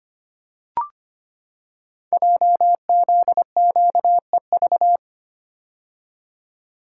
Morse code, heard as a keyed tone, about 25 wpm, 700 Hz. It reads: JZQEV
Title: Code Groups - 5 Letters - 25wpm